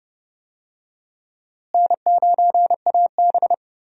Morse code, heard as a keyed tone, about 30 wpm, 700 Hz. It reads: N9AB
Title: USA Call Signs - 4 Characters - Repeated 2x - 30wpm